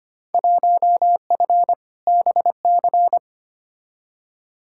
Morse code, heard as a keyed tone, about 25 wpm, 700 Hz. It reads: 1F BC
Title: Field Day Exchanges Encore - 25wpm